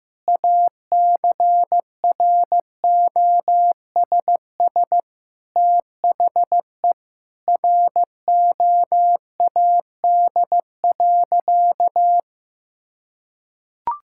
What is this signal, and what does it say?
Morse code, heard as a keyed tone, about 16 wpm, 700 Hz. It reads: ACROSS THE ROAD.
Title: Sentences from Top 1000 Words - Review Part 5 - 15wpm